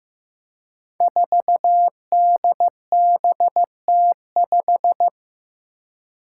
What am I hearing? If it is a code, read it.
Morse code, heard as a keyed tone, about 15 wpm, 700 Hz. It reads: 4DBT5
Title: Five Challenging Letter-Number Combinations 15wpm